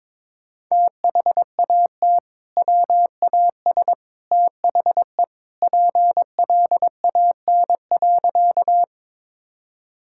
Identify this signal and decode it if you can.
Morse code, heard as a keyed tone, about 22 wpm, 700 Hz. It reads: THAT WAS THE PLAN.
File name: practice / 22wpm / Sentences with 2-Word Phrases - 22wpm